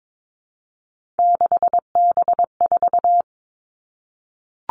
Morse code, heard as a keyed tone, about 22 wpm, 700 Hz. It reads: 6B4
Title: All Cut Numbers - 3-Digit 22wpm